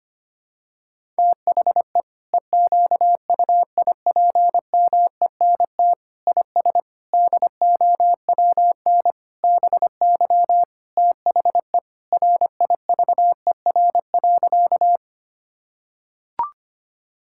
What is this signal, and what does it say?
Morse code, heard as a keyed tone, about 25 wpm, 700 Hz. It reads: THE EQUIPMENT IS DOWN BY THE RIVER.
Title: Sentences from Top 1000 Words - Review Part 5 - 25wpm